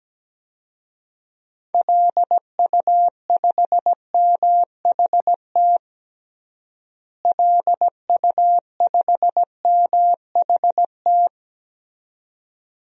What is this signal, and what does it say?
Morse code, heard as a keyed tone, about 17 wpm, 700 Hz. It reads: LU5MHT LU5MHT
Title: DXCC-100 Call Signs - 6 Characters - Repeated 2x - 17wpm